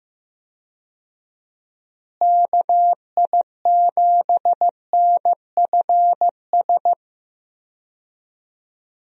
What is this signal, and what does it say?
Morse code, heard as a keyed tone, about 15 wpm, 700 Hz. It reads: KI7NFS
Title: USA Call Signs - 6 Characters - Repeated 2x - 15wpm